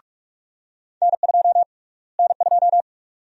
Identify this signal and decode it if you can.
Morse code, heard as a keyed tone, about 46 wpm, 700 Hz. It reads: N2 N2
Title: USA Call Sign Prefixes - Repeated 2x - 45wpm